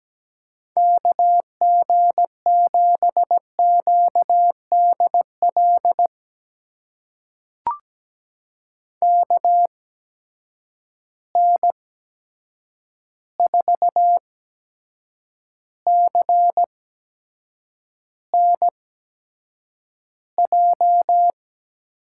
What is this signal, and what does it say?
Morse code, heard as a keyed tone, about 17 wpm, spaced as for 7 wpm, 700 Hz. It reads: KG7QDL KN4CNJ
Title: USA Call Signs - 6 Characters - 8x Character Spacing - 17wpm